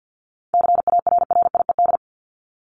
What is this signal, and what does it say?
Morse code, heard as a keyed tone, about 50 wpm, 700 Hz. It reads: CARRIED
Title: ED Suffix - Top 100 Words with Base Words 50wpm